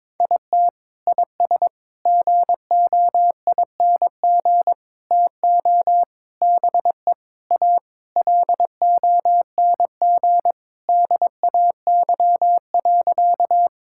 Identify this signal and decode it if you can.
Morse code, heard as a keyed tone, about 22 wpm, 700 Hz. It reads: IT IS GOING TO BE A LONG DAY.
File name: practice / 22wpm / Sentences with 4-Word Phrases - 22wpm